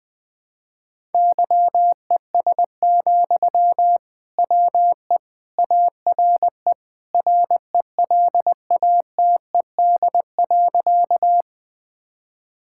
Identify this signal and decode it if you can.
Morse code, heard as a keyed tone, about 21 wpm, 700 Hz. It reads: YES, WE ARE RELATED.
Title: Sentences from Top 1000 Words - Review Part 5 - 20wpm